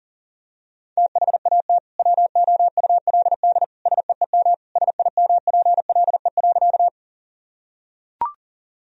Morse code, heard as a keyed tone, about 40 wpm, 700 Hz. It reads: THAT WOULD SEEM SIMPLE.
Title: Sentences from Top 1000 Words - Review Part 1 - 40wpm